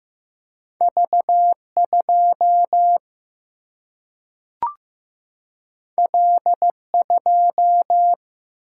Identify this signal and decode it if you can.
Morse code, heard as a keyed tone, about 15 wpm, 700 Hz. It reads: V2 L2
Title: Single-Letter Single-Number 15wpm